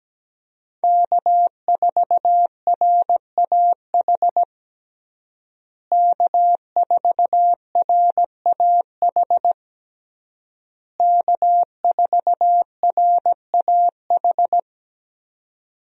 Morse code, heard as a keyed tone, about 17 wpm, 700 Hz. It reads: K4RAH K4RAH K4RAH
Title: USA Call Signs - 5 Characters - Repeated 3x - 17wpm